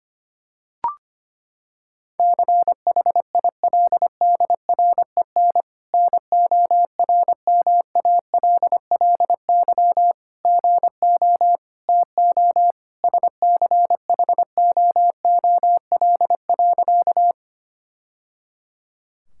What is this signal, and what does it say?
Morse code, heard as a keyed tone, about 25 wpm, 700 Hz. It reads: CHILDREN NORMALLY GO TO SCHOOL.
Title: Sentences from Top 1000 Words - Review Part 6 - 25wpm